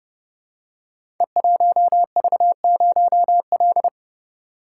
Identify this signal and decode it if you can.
Morse code, heard as a keyed tone, about 29 wpm, 700 Hz. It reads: E1V0L
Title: Code Groups - 5 Letter:Numbers - 30wpm